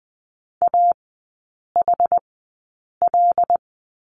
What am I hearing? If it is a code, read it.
Morse code, heard as a keyed tone, about 20 wpm, spaced as for 8 wpm, 700 Hz. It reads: AHL
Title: 074 Lesson 13e - Letter L - Three Characters 20wpm